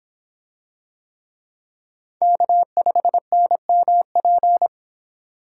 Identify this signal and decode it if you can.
Morse code, heard as a keyed tone, about 25 wpm, 700 Hz. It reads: K5NMP